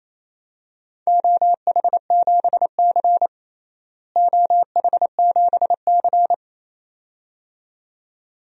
Morse code, heard as a keyed tone, about 28 wpm, 700 Hz. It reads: OH7C OH7C
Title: DXCC-100 Call Signs - 4 Characters - Repeated 2x - 28wpm